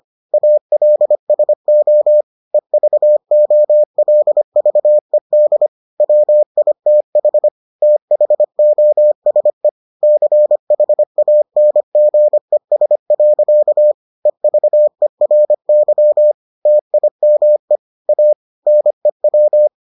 Morse code, heard as a keyed tone, about 25 wpm, 600 Hz. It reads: ALSO EVOLVED WITH THOSE CHANGES. EVERY TIME A NEW